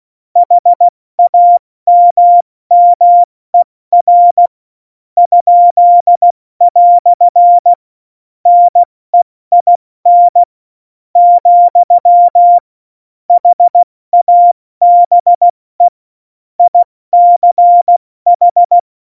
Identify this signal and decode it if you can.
Morse code, heard as a keyed tone, about 16 wpm, spaced as for 13 wpm, 700 Hz. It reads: HAMMER?" NEIN, HABE ICH
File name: Morsekurs / Witze / Witz 11. September 2018, 70 F80